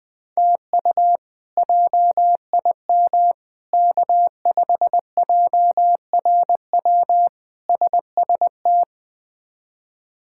Morse code, heard as a keyed tone, about 20 wpm, 700 Hz. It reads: TU JIM K5JRW SST
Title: K1USN SST QSO Practice - 20wpm